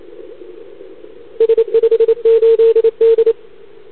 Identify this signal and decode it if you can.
Morse code, heard as a keyed tone, about 25 wpm, 440 Hz. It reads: S58D
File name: CQWWCW2021 / Doderna OL3Z